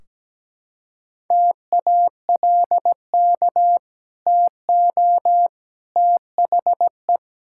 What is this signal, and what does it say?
Morse code, heard as a keyed tone, about 17 wpm, 700 Hz. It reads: TALK TO THE
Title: Sentences from Top 1600 Words 17wpm